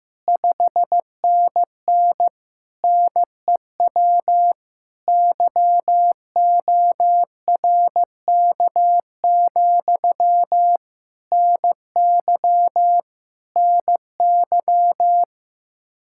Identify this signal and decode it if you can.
Morse code, heard as a keyed tone, about 15 wpm, 700 Hz. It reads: 5NN NEW YORK, NY NY
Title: Practice QSO - Exchange 1 - Beginning - 15wpm